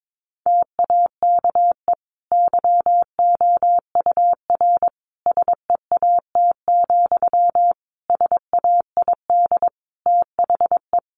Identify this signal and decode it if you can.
Morse code, heard as a keyed tone, about 22 wpm, 700 Hz. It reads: TAKE YOUR SEAT, SAID THE
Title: Sentences from Top 1500 Words 22wpm